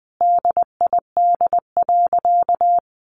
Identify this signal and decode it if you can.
Morse code, heard as a keyed tone, about 20 wpm, 700 Hz. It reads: DID.
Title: Sentences from Top 100 Words 20wpm